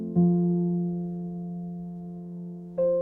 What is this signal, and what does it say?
Dial tone (North America), 350 + 440 Hz.